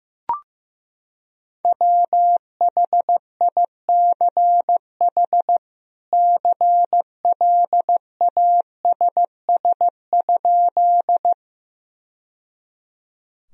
Morse code, heard as a keyed tone, about 15 wpm, 700 Hz. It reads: WHICH CLASS?
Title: Sentences from Top 1000 Words - Review Part 2 - 15wpm